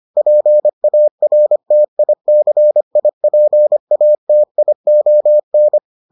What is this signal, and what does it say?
Morse code, heard as a keyed tone, about 25 wpm, 600 Hz. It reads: PARTICIPATION